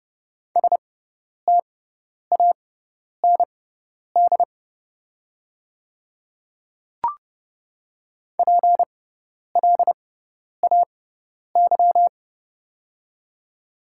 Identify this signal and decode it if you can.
Morse code, heard as a keyed tone, about 30 wpm, spaced as for 10 wpm, 690 Hz. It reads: STAND PLAY
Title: 6x Character Spacing - Top 300 Words - 30wpm